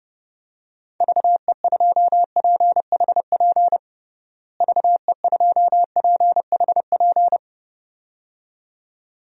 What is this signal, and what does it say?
Morse code, heard as a keyed tone, about 30 wpm, 700 Hz. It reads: VE2PHP VE2PHP
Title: DXCC-100 Call Signs - 6 Characters - Repeated 2x - 30wpm